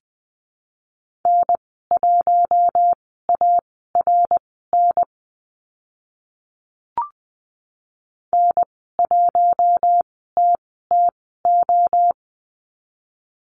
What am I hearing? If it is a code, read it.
Morse code, heard as a keyed tone, about 20 wpm, spaced as for 14 wpm, 700 Hz. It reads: N1ARN N1TTO
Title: 062 Lesson 11g - Letter h - Call Signs 20wpm